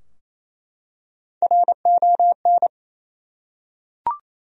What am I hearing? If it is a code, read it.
Morse code, heard as a keyed tone, about 28 wpm, 700 Hz. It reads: RON
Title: USA Common Names - Max 3 Letters - 28wpm